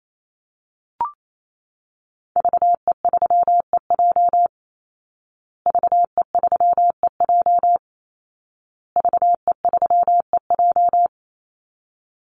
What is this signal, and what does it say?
Morse code, heard as a keyed tone, about 28 wpm, 700 Hz. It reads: VE3EJ VE3EJ VE3EJ
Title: Call Signs Repeated 2x - 28wpm